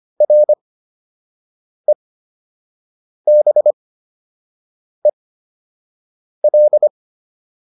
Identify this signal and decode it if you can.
Morse code, heard as a keyed tone, about 25 wpm, spaced as for 6 wpm, 600 Hz. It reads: REBEL